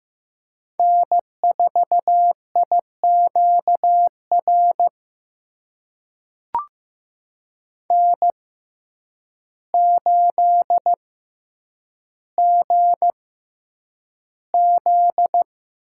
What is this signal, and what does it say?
Morse code, heard as a keyed tone, about 15 wpm, spaced as for 8 wpm, 700 Hz. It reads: N4IQR N8GZ
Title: USA Call Signs - 5 Characters - 6x Character Spacing - 15wpm